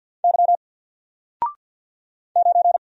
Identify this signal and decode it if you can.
Morse code, heard as a keyed tone, about 50 wpm, 700 Hz. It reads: Y 9
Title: Single Character (Letter:Number:Punctuation:Prosign) 50wpm